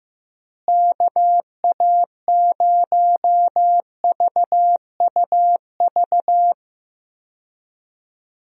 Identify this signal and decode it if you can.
Morse code, heard as a keyed tone, about 15 wpm, 700 Hz. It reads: KA0VUV
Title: USA Call Signs - 6 Characters - Repeated 2x - 15wpm